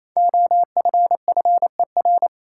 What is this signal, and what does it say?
Morse code, heard as a keyed tone, about 28 wpm, 700 Hz. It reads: OFFER
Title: Five Letter Words - 28wpm